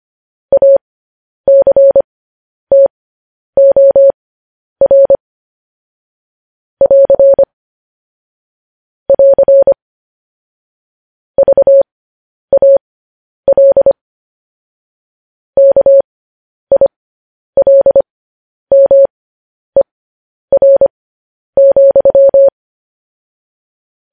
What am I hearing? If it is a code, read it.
Morse code, heard as a keyed tone, about 25 wpm, spaced as for 10 wpm, 550 Hz. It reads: ACTOR <AR> <AR> VAL KILMER,
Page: news headlines in morse code at 10 wpm